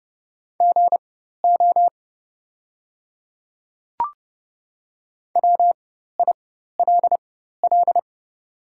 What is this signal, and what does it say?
Morse code, heard as a keyed tone, about 31 wpm, spaced as for 14 wpm, 700 Hz. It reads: GO WILL
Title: 4x Character Spacing - Top 100 Words - 30wpm